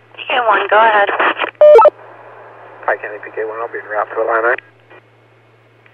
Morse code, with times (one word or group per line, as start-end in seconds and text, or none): none